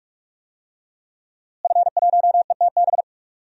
Morse code, heard as a keyed tone, about 46 wpm, 700 Hz. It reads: F1ETB
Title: DXCC-100 Call Signs - 5 Characters - Repeated 3x - 45wpm